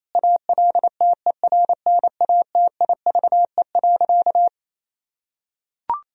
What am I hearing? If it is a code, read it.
Morse code, heard as a keyed tone, about 28 wpm, 700 Hz. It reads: ALTERNATIVE.
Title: Sentences from Top 1000 Words - Review Part 4 - 28wpm